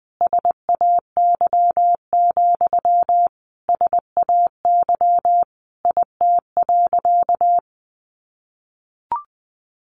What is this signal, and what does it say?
Morse code, heard as a keyed tone, about 20 wpm, 700 Hz. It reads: SAY, SAY IT.
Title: Sentences from Top 1000 Words 20wpm